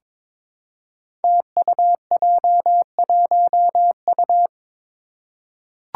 Morse code, heard as a keyed tone, about 22 wpm, 700 Hz. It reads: TUJ1U
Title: Five Challenging Letter-Number Combinations 22wpm